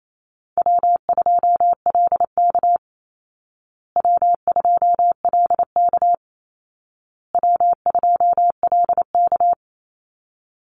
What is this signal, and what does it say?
Morse code, heard as a keyed tone, about 28 wpm, 700 Hz. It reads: W2LK W2LK W2LK
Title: Easy Call Signs Repeated 3x - 28wpm